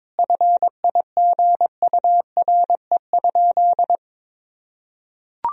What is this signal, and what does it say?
Morse code, heard as a keyed tone, about 22 wpm, 700 Hz. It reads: FIGURE?
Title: Sentences from Top 1000 Words - Review Part 2 - 22wpm